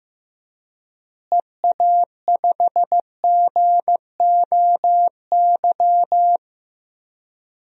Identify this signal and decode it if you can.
Morse code, heard as a keyed tone, about 16 wpm, 700 Hz. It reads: EA5GOY
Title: DXCC-100 Call Signs - 6 Characters - Repeated 3x - 15wpm